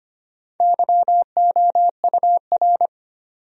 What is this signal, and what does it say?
Morse code, heard as a keyed tone, about 25 wpm, 700 Hz. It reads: YOUR
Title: Top 100 Words Repeated 25wpm